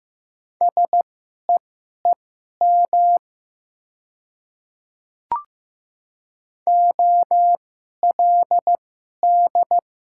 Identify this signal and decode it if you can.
Morse code, heard as a keyed tone, about 15 wpm, spaced as for 11 wpm, 700 Hz. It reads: SEEM OLD
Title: 2x Character Spacing - Top 300 Words - 15wpm